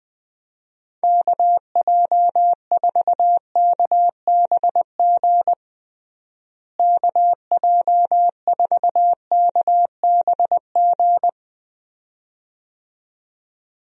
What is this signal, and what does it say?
Morse code, heard as a keyed tone, about 20 wpm, 700 Hz. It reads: KJ4KBG KJ4KBG
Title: USA Call Signs - 6 Characters - Repeated 2x - 20wpm